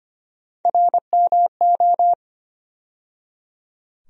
Morse code, heard as a keyed tone, about 25 wpm, 700 Hz. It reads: RMO
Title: USA Call Sign Postfixes - 3 Letters 25wpm